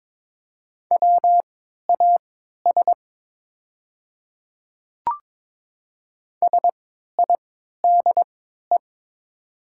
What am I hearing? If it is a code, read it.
Morse code, heard as a keyed tone, about 22 wpm, spaced as for 12 wpm, 700 Hz. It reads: WAS SIDE
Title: 3x Character Spacing - Top 100 Words - 22wpm